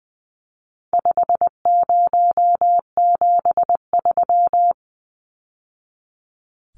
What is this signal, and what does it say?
Morse code, heard as a keyed tone, about 20 wpm, 700 Hz. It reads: 5073
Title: Numbers - 4-Digit 20wpm